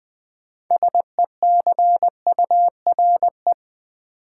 Morse code, heard as a keyed tone, about 20 wpm, 700 Hz. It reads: SECURE
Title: Sentences from Top 1200 Words - New Word Drill Alt - 20wpm